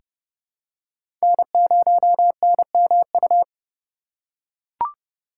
Morse code, heard as a keyed tone, about 31 wpm, 700 Hz. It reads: N0NMU
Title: USA Call Signs - 5 Characters - 2x Character Spacing - 30wpm